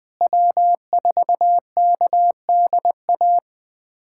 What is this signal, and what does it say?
Morse code, heard as a keyed tone, about 20 wpm, 700 Hz. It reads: W4KDA
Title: USA Callsign DE Callsign - 5 Characters - 20wpm